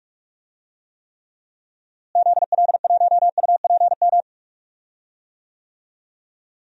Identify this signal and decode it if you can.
Morse code, heard as a keyed tone, about 46 wpm, 700 Hz. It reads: ZL1UPM
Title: DXCC-100 Call Signs - 6 Characters - Repeated 3x - 45wpm